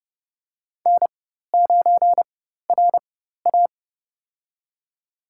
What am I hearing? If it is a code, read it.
Morse code, heard as a keyed tone, about 30 wpm, spaced as for 14 wpm, 690 Hz. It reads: N9RA
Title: USA Call Signs - 4x Character Spacing - 30wpm